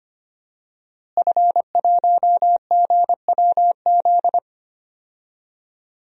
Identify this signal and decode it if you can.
Morse code, heard as a keyed tone, about 25 wpm, 700 Hz. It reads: F1GWZ
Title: DXCC-100 Call Signs - 5 Characters - Repeated 2x - 25wpm